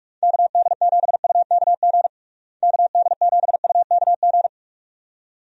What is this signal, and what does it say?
Morse code, heard as a keyed tone, about 42 wpm, 700 Hz. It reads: KD7UKG KD7UKG
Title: USA Call Signs - 6 Characters - Repeated 2x - 45wpm